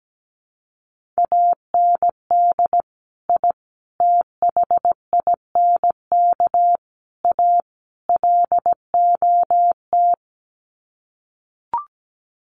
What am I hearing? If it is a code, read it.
Morse code, heard as a keyed tone, about 17 wpm, 700 Hz. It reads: AND I THINK A LOT
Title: Sets of 5 Words Encore2 17wpm